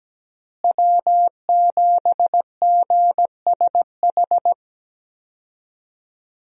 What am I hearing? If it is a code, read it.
Morse code, heard as a keyed tone, about 17 wpm, 700 Hz. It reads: W7GSH
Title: USA Call Signs - 5 Characters - Repeated 2x - 17wpm